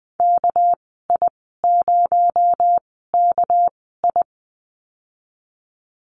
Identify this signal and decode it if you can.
Morse code, heard as a keyed tone, about 19 wpm, 700 Hz. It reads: K I 0 K I
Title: 238 Lesson 38g - Letter q - Call Signs 20wpm